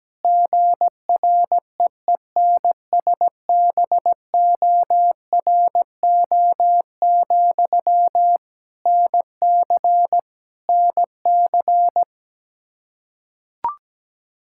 Morse code, heard as a keyed tone, about 17 wpm, 700 Hz. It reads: GREENSBORO, NC NC
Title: K1USN SST QSO Practice 17wpm